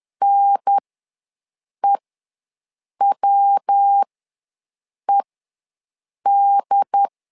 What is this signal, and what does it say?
Morse code, heard as a keyed tone, about 10 wpm, spaced as for 6 wpm, 800 Hz. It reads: NEWED
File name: LEZIONI / 3 MWRO / 3 MW ws 10wpm 2213